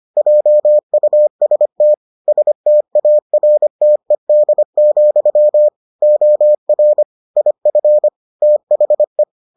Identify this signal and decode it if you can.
Morse code, heard as a keyed tone, about 25 wpm, 600 Hz. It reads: JUST STARTED, OR IF THE